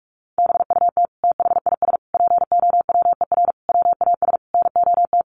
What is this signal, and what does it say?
Morse code, heard as a keyed tone, about 44 wpm, 700 Hz. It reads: BUT THIS POWER WAS NOT
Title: Sentences from Top 1500 Words 45wpm